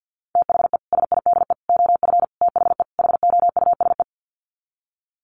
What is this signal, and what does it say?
Morse code, heard as a keyed tone, about 50 wpm, 700 Hz. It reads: THE SIDE OF THE HOUSE
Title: Sets of 5 Words Encore2 50wpm